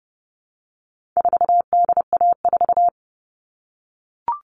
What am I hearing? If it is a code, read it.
Morse code, heard as a keyed tone, about 30 wpm, 700 Hz. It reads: 4DA4
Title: All Cut Numbers - 4-Digit 30wpm